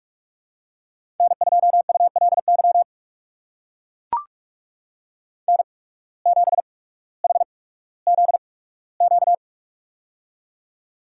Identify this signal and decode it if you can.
Morse code, heard as a keyed tone, about 43 wpm, spaced as for 19 wpm, 700 Hz. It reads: N2ULY N7HZQ
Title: USA Call Signs - 5 Characters - 8x Character Spacing - 45wpm